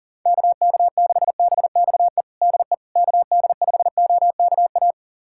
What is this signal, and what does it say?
Morse code, heard as a keyed tone, about 37 wpm, 700 Hz. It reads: KK6BXE DE KD5OKA
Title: USA Callsign DE Callsign - 6 Characters - 40wpm